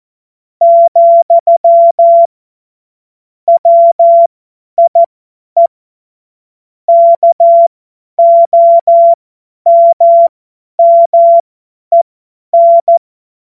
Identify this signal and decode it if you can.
Morse code, heard as a keyed tone, about 14 wpm, spaced as for 10 wpm, 680 Hz. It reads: , WIE KOMMEN